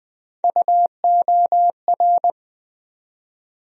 Morse code, heard as a keyed tone, about 20 wpm, 700 Hz. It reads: UOR
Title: USA Call Sign Postfixes - 3 Letters 20wpm